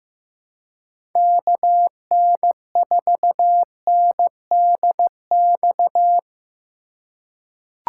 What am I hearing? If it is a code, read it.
Morse code, heard as a keyed tone, about 15 wpm, 700 Hz. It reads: KN4NDX
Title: USA Call Signs - 3x Character Spacing - 15wpm